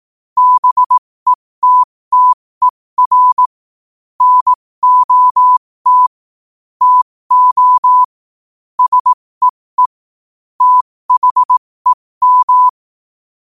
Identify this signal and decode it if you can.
Morse code, heard as a keyed tone, about 18 wpm, spaced as for 15 wpm, 1000 Hz. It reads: BETTER NOT TO SEE THEM
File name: qotdfiles / 15wpm